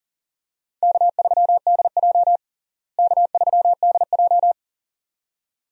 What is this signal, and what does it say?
Morse code, heard as a keyed tone, about 40 wpm, 700 Hz. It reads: K3DJ K3DJ
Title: USA Call Signs - 4 Characters - Repeated 2x - 40wpm